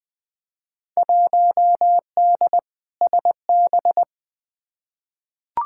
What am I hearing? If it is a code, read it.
Morse code, heard as a keyed tone, about 20 wpm, 700 Hz. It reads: 1D SB
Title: Field Day Exchanges - 20wpm